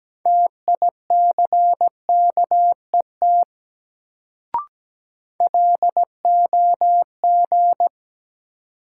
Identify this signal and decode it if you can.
Morse code, heard as a keyed tone, about 17 wpm, 700 Hz. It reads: TICKET LOG
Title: Sentences from Top 1400 Words - New Word Drill Alt - 17wpm